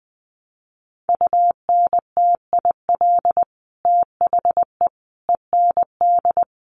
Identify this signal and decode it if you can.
Morse code, heard as a keyed tone, about 20 wpm, 700 Hz. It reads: UNTIL THE END
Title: Sets of 5 Words Encore 20wpm